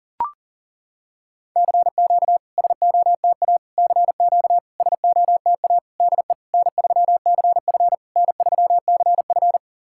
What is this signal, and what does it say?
Morse code, heard as a keyed tone, about 38 wpm, 700 Hz. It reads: CQ SOTA CQ SOTA DE N3CF N3CF
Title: SOTA QSO Practice 40wpm